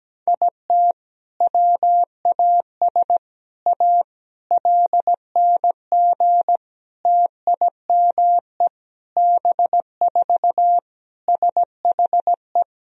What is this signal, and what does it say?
Morse code, heard as a keyed tone, about 17 wpm, 700 Hz. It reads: IT WAS A LNG TIME B4 SHE
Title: Sentences from Top 500 Words and Abbreviations 17wpm